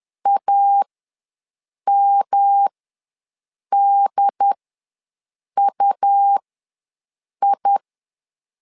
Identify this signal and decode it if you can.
Morse code, heard as a keyed tone, about 10 wpm, spaced as for 6 wpm, 800 Hz. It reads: AMDUI
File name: LEZIONI / 3 MWRO / 3 MW ws 10wpm 2213